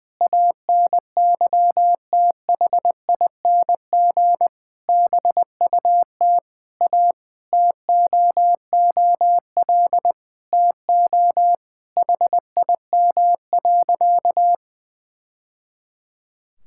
Morse code, heard as a keyed tone, about 20 wpm, 700 Hz. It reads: ANYTHING BUT A TOOL TO HIM.
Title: Sentences from Top 1000 Words - Review Part 6 - 20wpm